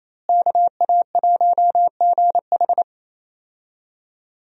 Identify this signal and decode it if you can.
Morse code, heard as a keyed tone, about 28 wpm, 700 Hz. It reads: KA1GH
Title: USA Call Signs - 5 Characters - Repeated 3x - 28wpm